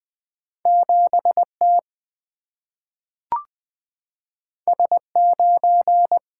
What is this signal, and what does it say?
Morse code, heard as a keyed tone, about 20 wpm, 700 Hz. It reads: 7T S9
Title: Single-Letter Single-Number 20wpm